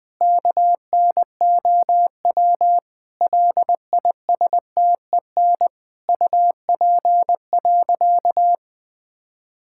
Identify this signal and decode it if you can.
Morse code, heard as a keyed tone, about 20 wpm, 700 Hz. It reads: KNOW LISTEN UP.
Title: Sentences with 7-Word Phrases - Spoken Beginning - 20wpm